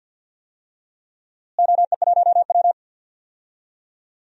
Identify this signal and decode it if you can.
Morse code, heard as a keyed tone, about 46 wpm, 700 Hz. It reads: OE1W
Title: DXCC-100 Call Signs - 4 Characters - Repeated 2x - 50wpm